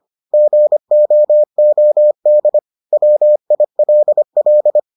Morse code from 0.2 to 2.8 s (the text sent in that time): GOOD